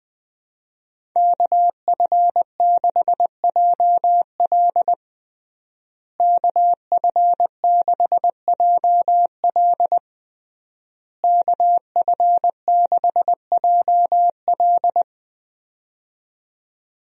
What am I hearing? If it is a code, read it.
Morse code, heard as a keyed tone, about 20 wpm, 700 Hz. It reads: KF6JL KF6JL KF6JL
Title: USA Call Signs - 5 Characters - Repeated 3x - 20wpm